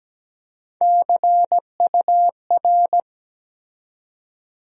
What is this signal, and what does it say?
Morse code, heard as a keyed tone, about 17 wpm, 700 Hz. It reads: CUR
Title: USA Call Sign Postfixes - 3 Letters 17wpm